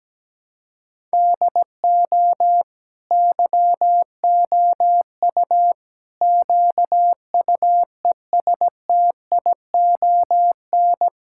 Morse code, heard as a keyed tone, about 17 wpm, 700 Hz. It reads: DO YOU QUESTION
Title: Sentences from Top 1000 Words - Review Part 5 - 17wpm